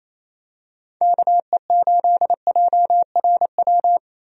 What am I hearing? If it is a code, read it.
Morse code, heard as a keyed tone, about 28 wpm, 700 Hz. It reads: KE8JRW
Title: USA Call Signs - 6 Characters - 28wpm